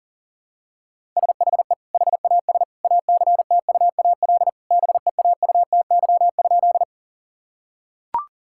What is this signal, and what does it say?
Morse code, heard as a keyed tone, about 40 wpm, 700 Hz. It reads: SHE HAS ACTUAL BEAUTY?
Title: Sentences from Top 1000 Words - Review Part 5 - 40wpm